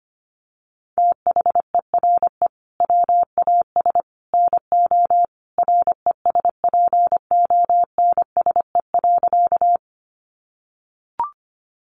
Morse code, heard as a keyed tone, about 25 wpm, 700 Hz. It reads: THERE WAS NO RESPONSE.
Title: Sentences from Top 1400 Words 25wpm